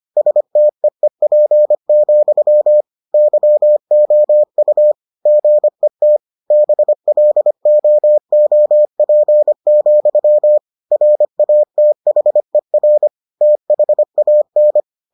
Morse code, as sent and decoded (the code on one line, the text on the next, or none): ... - . .--. --..-- / -.-- --- ..- / --. - / -... .-.. --- --- .--. --..-- / .-. .- - .... .-. / - .... .- -.
STEEP, YOU GET BLOOP, RATHER THAN